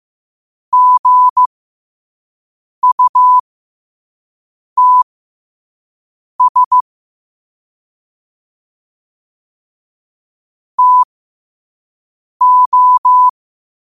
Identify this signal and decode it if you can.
Morse code, heard as a keyed tone, about 15 wpm, spaced as for 5 wpm, 1000 Hz. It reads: GUTS TO
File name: qotdfiles / 5wpm